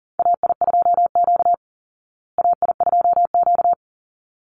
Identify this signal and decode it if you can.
Morse code, heard as a keyed tone, about 40 wpm, 700 Hz. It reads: AI2Q AI2Q
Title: Call Signs Repeated 3x - 40wpm